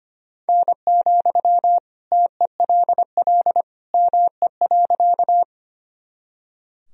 Morse code, heard as a keyed tone, about 25 wpm, 700 Hz. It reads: N, TELL ME.
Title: Sentences from Top 200 Words and Abbreviations 25wpm